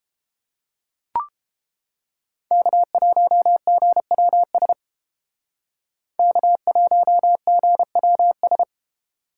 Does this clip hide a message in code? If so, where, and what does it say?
Yes: Morse code from 2.5 to 8.6 s, heard as a keyed tone, about 29 wpm, 700 Hz. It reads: K1GWS K1GWS